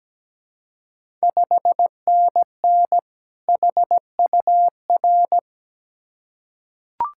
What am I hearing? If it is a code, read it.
Morse code, heard as a keyed tone, about 17 wpm, 700 Hz. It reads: 5NN HUR
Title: Ontario QSO Party Exchange 17wpm